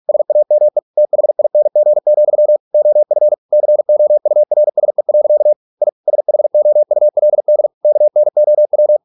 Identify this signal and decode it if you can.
Morse code, heard as a keyed tone, about 46 wpm, 600 Hz. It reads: SAME THING, OF COURSE. I SHOULD KNOW